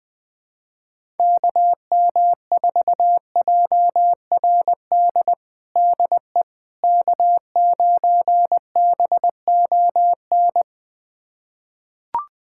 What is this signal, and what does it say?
Morse code, heard as a keyed tone, about 20 wpm, 700 Hz. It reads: KM4JRD DE K9BON
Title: USA Callsign DE Callsign 20wpm